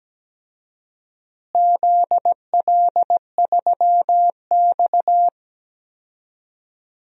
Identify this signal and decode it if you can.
Morse code, heard as a keyed tone, about 17 wpm, 700 Hz. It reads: ZL3X
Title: DXCC-100 Call Signs - 4 Characters - Repeated 2x - 17wpm